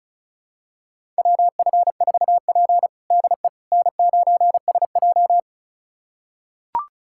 Morse code, heard as a keyed tone, about 36 wpm, 700 Hz. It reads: WF4P DE N9SJ
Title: USA Callsign DE Callsign - 4 Characters - 35wpm